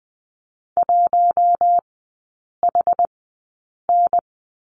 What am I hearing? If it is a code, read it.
Morse code, heard as a keyed tone, about 20 wpm, spaced as for 8 wpm, 700 Hz. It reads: 1HN
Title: 060 Lesson 11e - Letter h - Three Characters 20wpm